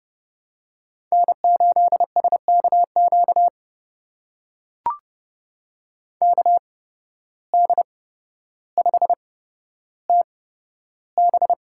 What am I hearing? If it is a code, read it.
Morse code, heard as a keyed tone, about 30 wpm, spaced as for 13 wpm, 700 Hz. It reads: N8SKQ KD5TB
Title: USA Call Signs - 5 Characters - 8x Character Spacing - 30wpm